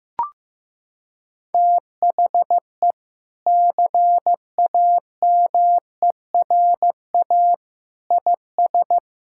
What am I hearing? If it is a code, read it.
Morse code, heard as a keyed tone, about 16 wpm, 700 Hz. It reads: THE CAMERA IS
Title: Sentences from Top 1000 Words - Review Part 5 - 15wpm